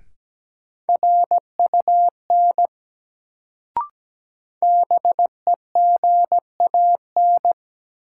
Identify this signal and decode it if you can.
Morse code, heard as a keyed tone, about 17 wpm, 700 Hz. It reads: RUN BEGAN
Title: Sentences from Top 300 Words - New Word Drill - 17wpm